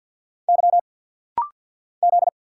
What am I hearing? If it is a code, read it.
Morse code, heard as a keyed tone, about 50 wpm, 690 Hz. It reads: Y Z
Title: Single Letters - Rapid-Fire 50wpm